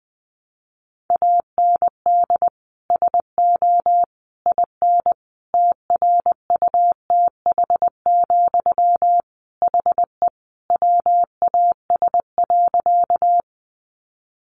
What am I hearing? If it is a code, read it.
Morse code, heard as a keyed tone, about 20 wpm, 700 Hz. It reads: AND SO IN TRUTH, HE WAS.
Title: Sentences from Top 800 Words 20wpm